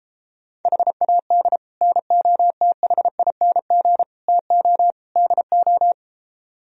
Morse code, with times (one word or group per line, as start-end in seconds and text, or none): 0.7-1.6 HAD
1.8-4.0 NOTHING
4.3-4.9 TO
5.2-5.9 DO